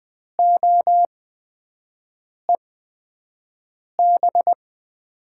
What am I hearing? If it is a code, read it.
Morse code, heard as a keyed tone, about 20 wpm, spaced as for 5 wpm, 700 Hz. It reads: OEB
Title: USA Call Signs - 8x Character Spacing - 20wpm